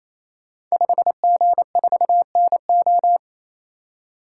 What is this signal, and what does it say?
Morse code, heard as a keyed tone, about 28 wpm, 700 Hz. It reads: 5G4NO
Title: Code Groups - 5 Letter:Numbers - 28wpm